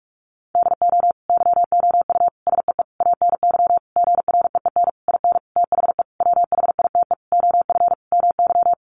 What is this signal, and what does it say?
Morse code, heard as a keyed tone, about 44 wpm, 700 Hz. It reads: DO YOU SEE ANY GREEN IN THE WHITE OF MY
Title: Sentences from Top 500 Words 45wpm